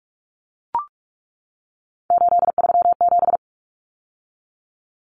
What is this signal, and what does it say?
Morse code, heard as a keyed tone, about 44 wpm, 700 Hz. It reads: 837
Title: Numbers - 3-Digit 45wpm